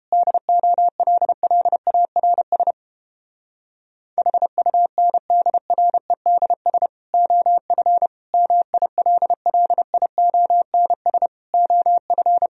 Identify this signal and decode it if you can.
Morse code, heard as a keyed tone, about 31 wpm, 700 Hz. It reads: DOLLARS HUNDREDS OF MILLIONS OF